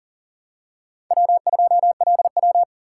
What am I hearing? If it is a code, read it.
Morse code, heard as a keyed tone, about 40 wpm, 700 Hz. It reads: W2LW
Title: USA Call Signs - 4 Characters - 4x Character Spacing - 40wpm